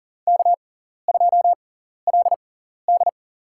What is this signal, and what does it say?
Morse code, heard as a keyed tone, about 37 wpm, spaced as for 14 wpm, 700 Hz. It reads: K2LD
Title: USA Call Signs - 4 Characters - 6x Character Spacing - 40wpm